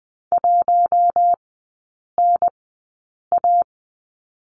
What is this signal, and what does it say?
Morse code, heard as a keyed tone, about 20 wpm, spaced as for 8 wpm, 700 Hz. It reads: 1NA
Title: 041 Lesson 8e - Number 1 - Three Characters 20wpm